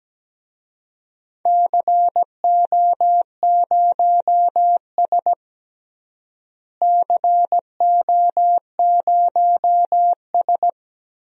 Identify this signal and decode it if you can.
Morse code, heard as a keyed tone, about 17 wpm, 700 Hz. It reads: CO0S CO0S
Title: DXCC-100 Call Signs - 4 Characters - Repeated 3x - 17wpm